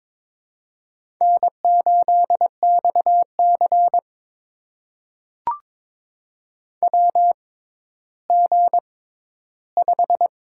Morse code, heard as a keyed tone, about 22 wpm, spaced as for 12 wpm, 700 Hz. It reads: N8XC WG5
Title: USA Call Signs - 6x Character Spacing - 22wpm